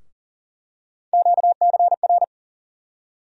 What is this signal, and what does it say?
Morse code, heard as a keyed tone, about 37 wpm, 700 Hz. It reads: QCR
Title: USA Call Sign Postfixes - 3 Letters 40wpm